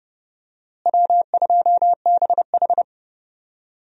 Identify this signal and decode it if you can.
Morse code, heard as a keyed tone, about 30 wpm, 700 Hz. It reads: W2BH
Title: USA Call Signs - 4 Characters - Repeated 3x - 30wpm